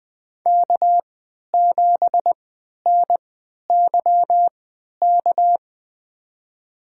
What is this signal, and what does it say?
Morse code, heard as a keyed tone, about 21 wpm, spaced as for 11 wpm, 700 Hz. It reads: K7NYK
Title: USA Call Signs - 5 Characters - 3x Character Spacing - 20wpm